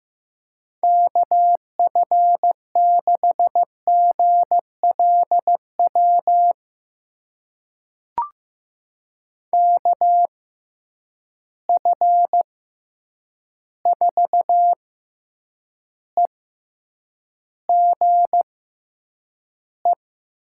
Morse code, heard as a keyed tone, about 15 wpm, spaced as for 8 wpm, 700 Hz. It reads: KF6GLW KF4EGE
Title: USA Call Signs - 6 Characters - 6x Character Spacing - 15wpm